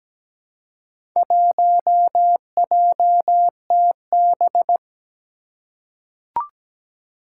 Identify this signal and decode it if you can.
Morse code, heard as a keyed tone, about 17 wpm, 700 Hz. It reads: 1JTB
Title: Four Challenging Letter-Number Combinations 17wpm